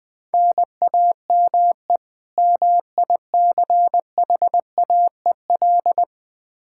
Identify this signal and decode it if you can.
Morse code, heard as a keyed tone, about 20 wpm, 700 Hz. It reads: NAME MICHAEL
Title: Practice QSO - Exchange 1 - Beginning - 20wpm